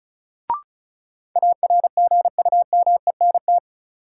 Morse code, heard as a keyed tone, about 34 wpm, 700 Hz. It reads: ARGUMENT
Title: Sentences from Top 1400 Words - New Word Drill Alt - 35wpm